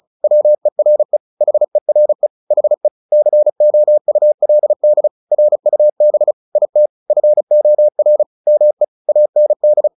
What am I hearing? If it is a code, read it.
Morse code, heard as a keyed tone, about 35 wpm, 600 Hz. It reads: WERE HERE HE COULD RUB IT FOR ME AND
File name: the old man and the sea / the old man and the sea 1 35wpm